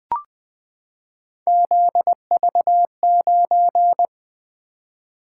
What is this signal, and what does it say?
Morse code, heard as a keyed tone, about 20 wpm, 700 Hz. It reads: ZV9
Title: Three Letter-Number Combinations - 20wpm